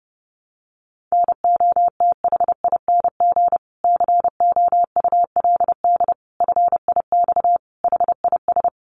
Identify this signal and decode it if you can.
Morse code, heard as a keyed tone, about 30 wpm, 700 Hz. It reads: NOTHING COULD FIX HIS
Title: Sentences from Top 1400 Words 30wpm